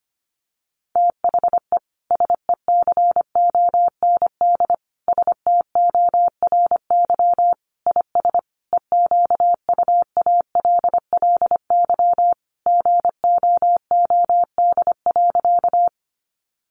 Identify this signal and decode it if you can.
Morse code, heard as a keyed tone, about 25 wpm, 700 Hz. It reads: THE SECOND STORY IS EQUALLY GOOD.
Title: Sentences from Top 1400 Words 25wpm